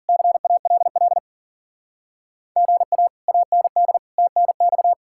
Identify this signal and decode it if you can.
Morse code, heard as a keyed tone, about 50 wpm, 700 Hz. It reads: CALL GA AND TNX
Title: Speed-Racing Common QSO Phrases Encore - 40wpm